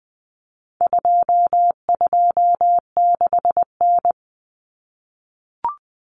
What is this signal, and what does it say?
Morse code, heard as a keyed tone, about 20 wpm, 700 Hz. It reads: 226N
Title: Common Cut Numbers - 4-Digit 20wpm